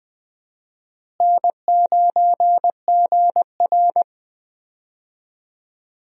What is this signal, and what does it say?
Morse code, heard as a keyed tone, about 21 wpm, 700 Hz. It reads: N9GR
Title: USA Call Signs - 4 Characters - Repeated 3x - 20wpm